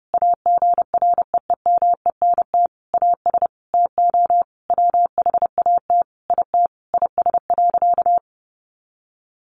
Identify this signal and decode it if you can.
Morse code, heard as a keyed tone, about 30 wpm, 700 Hz. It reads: AGREEMENT AS TO WHAT IT IS.
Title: Sentences from Top 1500 Words 30wpm